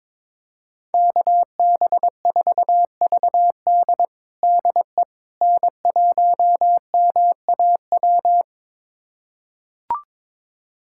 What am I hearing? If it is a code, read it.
Morse code, heard as a keyed tone, about 22 wpm, 700 Hz. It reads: KB4VD DE N1MAW
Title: USA Callsign DE Callsign - 5 Characters - 22wpm